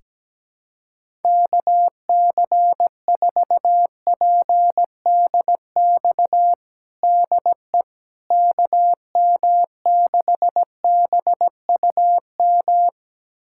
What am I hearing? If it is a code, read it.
Morse code, heard as a keyed tone, about 17 wpm, 700 Hz. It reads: KC4PDX DE KM6BUM
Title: USA Callsign DE Callsign 17wpm